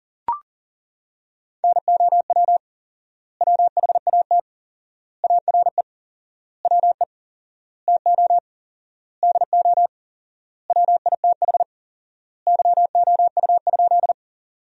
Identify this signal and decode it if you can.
Morse code, heard as a keyed tone, about 40 wpm, 700 Hz. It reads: NOW WHAT ARE WE TO DO WITH YOU?
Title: 4x Word-Spacing - Sentences from Top 100 Words - 40wpm